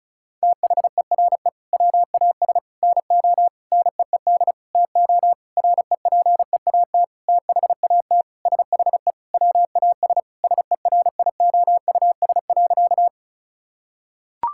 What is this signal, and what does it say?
Morse code, heard as a keyed tone, about 34 wpm, 700 Hz. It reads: THERE WAS NO NEED TO REPEAT THAT SHE WAS SERIOUS.
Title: Sentences from Top 1600 Words 35wpm